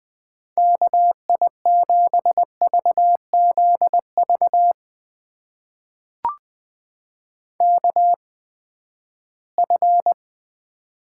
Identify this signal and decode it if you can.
Morse code, heard as a keyed tone, about 20 wpm, spaced as for 14 wpm, 700 Hz. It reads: KI7VZV KF
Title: USA Call Signs - 6 Characters - 8x Character Spacing - 20wpm